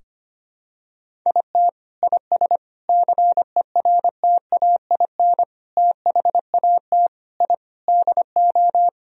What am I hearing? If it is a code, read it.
Morse code, heard as a keyed tone, about 25 wpm, 700 Hz. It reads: IT IS CERTAIN THAT I DO
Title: Sentences from Top 1000 Words - Review Part 2 - 25wpm